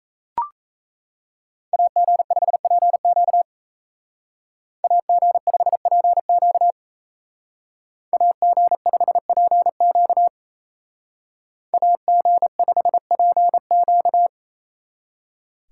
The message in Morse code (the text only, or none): AG5PQ AG5PQ AG5PQ AG5PQ